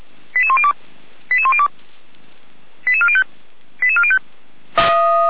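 Five-tone selective call (ZVEI), heard as quick runs of five tones, tones of about 80 ms. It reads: 88102 88102 88405 88405
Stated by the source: ZVEI five-tone selective call received on SDR.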